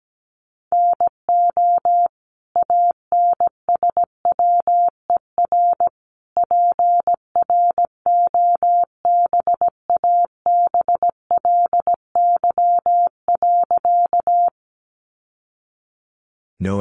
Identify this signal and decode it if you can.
Morse code, heard as a keyed tone, about 17 wpm, 700 Hz. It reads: NO ANSWER PROBABLY.
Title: Sentences from Top 700 Words 17wpm